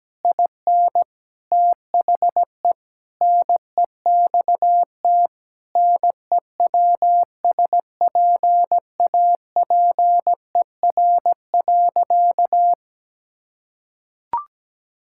Morse code, heard as a keyed tone, about 17 wpm, 700 Hz. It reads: IN THE NEXT NEWSPAPER.
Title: Sentences from Top 1000 Words - Review Part 6 - 17wpm